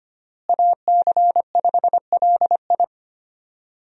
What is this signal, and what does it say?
Morse code, heard as a keyed tone, about 25 wpm, 700 Hz. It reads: AC5LI
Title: USA Call Signs - 5 Characters - Repeated 2x - 25wpm